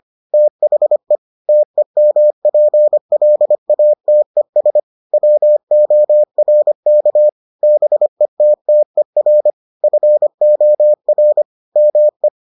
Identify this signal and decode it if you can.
Morse code, heard as a keyed tone, about 25 wpm, 600 Hz. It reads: THE TEMPLATES WORK BETTER FOR ME